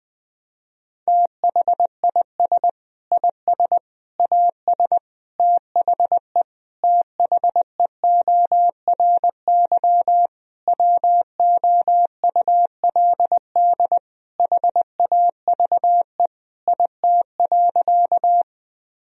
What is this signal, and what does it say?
Morse code, heard as a keyed tone, about 21 wpm, 700 Hz. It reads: THIS IS AS THE THEORY WOULD HAVE IT.
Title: Sentences from Top 1000 Words - Review Part 3 - 20wpm